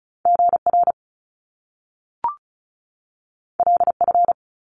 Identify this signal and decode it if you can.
Morse code, heard as a keyed tone, about 36 wpm, 700 Hz. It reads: GR LF
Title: Top Two Letter Combinations in Words 35wpm